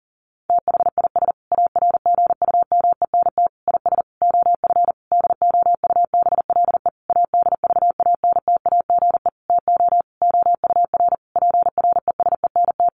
Morse code, heard as a keyed tone, about 40 wpm, 700 Hz. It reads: THIS ARGUMENT IS OF DOUBLE ADVANTAGE TO OUR PRESENT
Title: Sentences from Top 1400 Words 40wpm